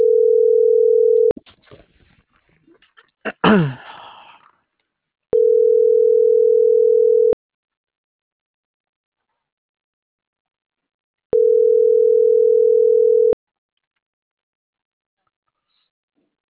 Ringback tone (North America), 440 + 480 Hz.